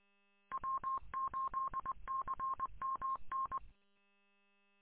Morse code, heard as a keyed tone, about 20 wpm, 1070 Hz. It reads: W8CMN
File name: w8cmn- -combined